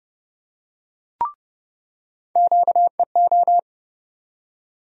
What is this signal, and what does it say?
Morse code, heard as a keyed tone, about 29 wpm, 700 Hz. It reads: QEO
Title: USA Call Sign Postfixes - 3 Letters 30wpm